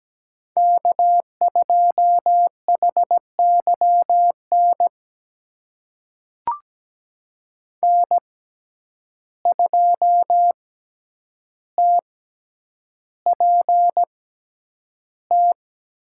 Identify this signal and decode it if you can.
Morse code, heard as a keyed tone, about 17 wpm, spaced as for 9 wpm, 700 Hz. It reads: K2HYN N2TPT
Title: USA Call Signs - 5 Characters - 6x Character Spacing - 17wpm